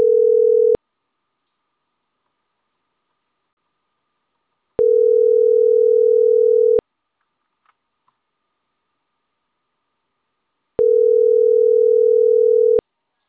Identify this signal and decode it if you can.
Ringback tone (North America), 440 + 480 Hz.